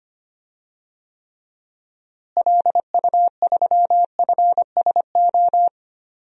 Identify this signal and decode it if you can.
Morse code, heard as a keyed tone, about 25 wpm, 700 Hz. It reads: LU3FSO
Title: DXCC-100 Call Signs - 6 Characters - Repeated 2x - 25wpm